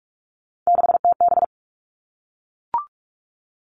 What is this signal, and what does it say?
Morse code, heard as a keyed tone, about 44 wpm, 700 Hz. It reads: 6TB
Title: All Cut Numbers - 3-Digit 45wpm